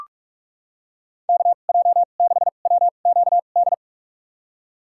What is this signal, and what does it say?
Morse code, heard as a keyed tone, about 46 wpm, spaced as for 32 wpm, 700 Hz. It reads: KJ6WQD
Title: USA Call Signs - 6 Characters - 2x Character Spacing - 45wpm